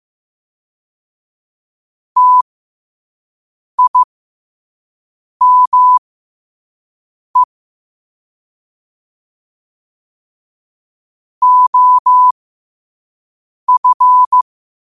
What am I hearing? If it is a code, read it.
Morse code, heard as a keyed tone, about 15 wpm, spaced as for 5 wpm, 1000 Hz. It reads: TIME OF
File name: 5wpm